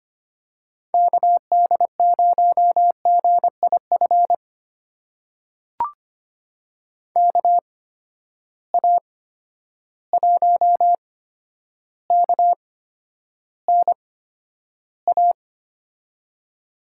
Morse code, heard as a keyed tone, about 25 wpm, spaced as for 11 wpm, 700 Hz. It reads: KD0GIF KA1KNA
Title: USA Call Signs - 6 Characters - 8x Character Spacing - 25wpm